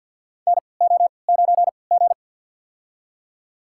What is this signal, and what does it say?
Morse code, heard as a keyed tone, about 53 wpm, spaced as for 29 wpm, 700 Hz. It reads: NO9G
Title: USA Call Signs - 4 Characters - 3x Character Spacing - 50wpm